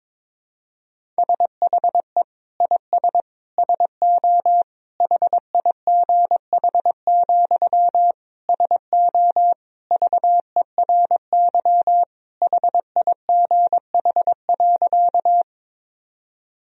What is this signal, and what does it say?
Morse code, heard as a keyed tone, about 22 wpm, 700 Hz. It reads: SHE IS SO HIGH, SO VERY HIGH.
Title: Sentences from Top 1000 Words - Review Part 1 - 22wpm